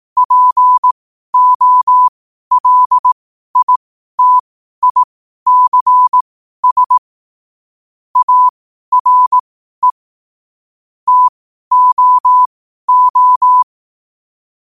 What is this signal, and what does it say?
Morse code, heard as a keyed tone, about 18 wpm, spaced as for 12 wpm, 1000 Hz. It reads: POLITICS ARE TOO
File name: qotdfiles / 12wpm